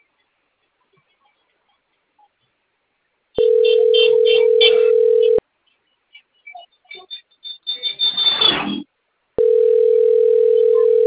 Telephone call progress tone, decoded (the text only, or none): ringback tone